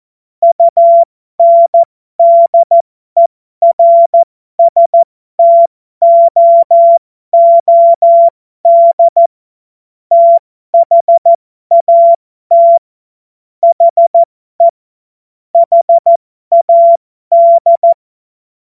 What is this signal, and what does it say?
Morse code, heard as a keyed tone, about 14 wpm, spaced as for 12 wpm, 680 Hz. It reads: UNDERSTOOD THAT HE HAD
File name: Morsekurs / Witze